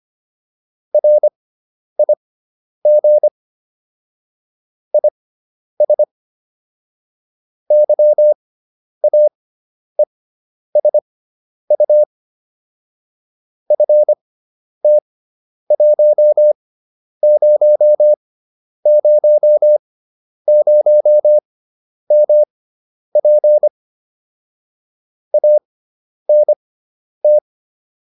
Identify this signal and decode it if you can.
Morse code, heard as a keyed tone, about 25 wpm, spaced as for 10 wpm, 600 Hz. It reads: RIG IS YAESU FT1000MP ANT